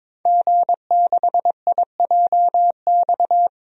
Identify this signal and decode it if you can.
Morse code, heard as a keyed tone, about 22 wpm, 700 Hz. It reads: G6IJX
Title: DXCC-100 Call Signs - 5 Characters - 22wpm